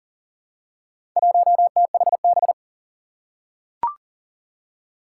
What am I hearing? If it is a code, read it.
Morse code, heard as a keyed tone, about 40 wpm, 700 Hz. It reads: 1THB
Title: Four Challenging Letter-Number Combinations 40wpm